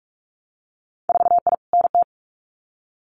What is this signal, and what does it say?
Morse code, heard as a keyed tone, about 46 wpm, 700 Hz. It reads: 4I NT
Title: Winter Field Day Exchanges Encore 45wpm